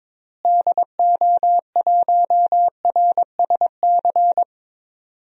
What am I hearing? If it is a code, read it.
Morse code, heard as a keyed tone, about 21 wpm, 700 Hz. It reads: DO1RSC
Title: DXCC-100 Call Signs - 6 Characters - Repeated 2x - 22wpm